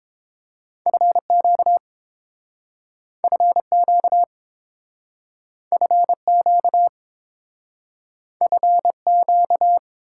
Morse code, heard as a keyed tone, about 30 wpm, 700 Hz. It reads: FQ FQ FQ FQ